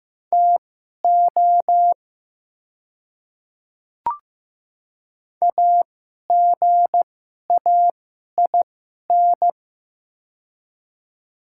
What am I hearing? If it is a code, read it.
Morse code, heard as a keyed tone, about 15 wpm, spaced as for 11 wpm, 700 Hz. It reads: TO AGAIN
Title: 2x Character Spacing - Top 300 Words - 15wpm